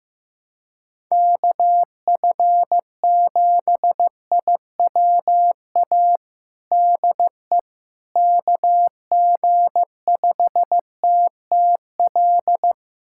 Morse code, heard as a keyed tone, about 15 wpm, 700 Hz. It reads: KF7IWA DE KG5TTL
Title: USA Callsign DE Callsign 15wpm